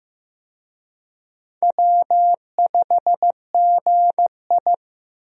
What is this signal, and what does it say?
Morse code, heard as a keyed tone, about 15 wpm, 700 Hz. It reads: W5GI
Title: USA Call Signs - 4 Characters - 15wpm